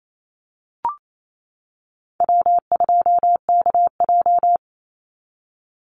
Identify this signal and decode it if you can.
Morse code, heard as a keyed tone, about 28 wpm, 700 Hz. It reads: W2KJ
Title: Call Signs - 28wpm